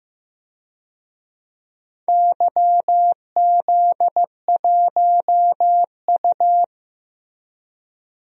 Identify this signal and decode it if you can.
Morse code, heard as a keyed tone, about 16 wpm, 700 Hz. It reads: YZ1U
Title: DXCC-100 Call Signs - 4 Characters - Repeated 2x - 15wpm